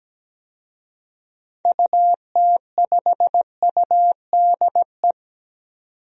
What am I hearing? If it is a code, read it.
Morse code, heard as a keyed tone, about 17 wpm, 700 Hz. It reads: UT5UDE
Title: DXCC-100 Call Signs - 6 Characters - 17wpm